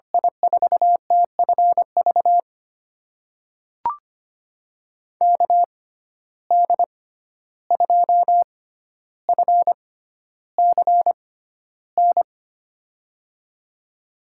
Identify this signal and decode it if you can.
Morse code, heard as a keyed tone, about 25 wpm, spaced as for 12 wpm, 700 Hz. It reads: I4TFV KD2FCN
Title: USA Call Signs - 6 Characters - 6x Character Spacing - 25wpm